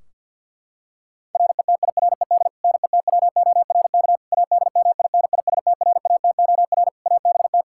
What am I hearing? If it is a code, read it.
Morse code, heard as a keyed tone, about 51 wpm, 700 Hz. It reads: RETIRED NETWORK ADMINISTRATOR ABT
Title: Practice QSO - Exchange 3 - More Info - 50wpm